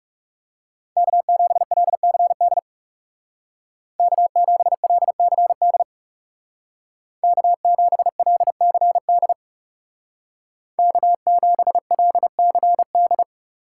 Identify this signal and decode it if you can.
Morse code, heard as a keyed tone, about 35 wpm, 700 Hz. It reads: K7LCD K7LCD K7LCD K7LCD
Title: Speed Racing - USA 1x3 Call Signs 30wpm